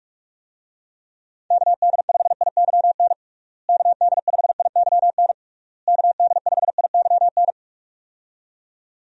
Morse code, heard as a keyed tone, about 46 wpm, 700 Hz. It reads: KD5IYN KD5IYN KD5IYN
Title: USA Call Signs - 6 Characters - Repeated 3x - 45wpm